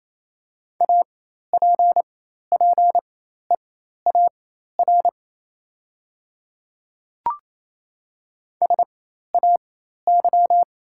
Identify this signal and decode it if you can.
Morse code, heard as a keyed tone, about 28 wpm, spaced as for 13 wpm, 700 Hz. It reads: APPEAR SAY
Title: 4x Character Spacing - Top 500 Words - 28wpm